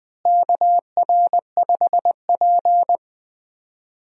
Morse code, heard as a keyed tone, about 20 wpm, 700 Hz. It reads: KR5P
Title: USA Call Signs - 4 Characters - Repeated 3x - 20wpm